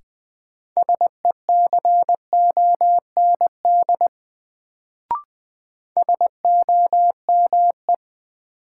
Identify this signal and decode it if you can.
Morse code, heard as a keyed tone, about 20 wpm, 700 Hz. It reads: SECOND SOME
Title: Sets of 2 Words - New Word Drill - 20wpm